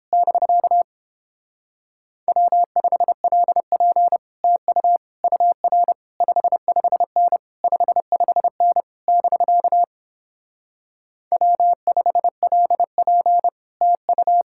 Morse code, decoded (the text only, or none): <BK> W5LP TU UR 55N 55N <BK> W5LP TU